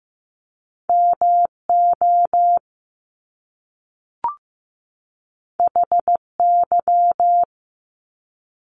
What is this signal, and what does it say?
Morse code, heard as a keyed tone, about 15 wpm, 700 Hz. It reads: MO HY
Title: Top Two Letter Combinations in Words 15wpm